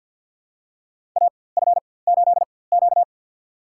Morse code, heard as a keyed tone, about 47 wpm, spaced as for 23 wpm, 700 Hz. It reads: AF8Q
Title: USA Call Signs - 4x Character Spacing - 50wpm